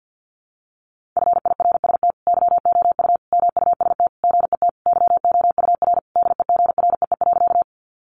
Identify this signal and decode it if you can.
Morse code, heard as a keyed tone, about 50 wpm, 700 Hz. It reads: FIRST YOU MUST GET YOUR DEGREE.
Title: Sentences from Top 1500 Words 50wpm